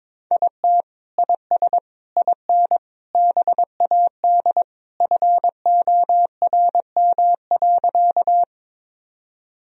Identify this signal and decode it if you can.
Morse code, heard as a keyed tone, about 22 wpm, 700 Hz. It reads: IT IS IN BAD FORM.
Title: Sentences from Top 1000 Words - Review Part 1 - 22wpm